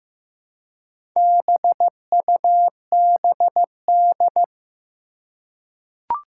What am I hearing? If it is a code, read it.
Morse code, heard as a keyed tone, about 15 wpm, 700 Hz. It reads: BUBD
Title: Four Challenging Letter-Number Combinations 15wpm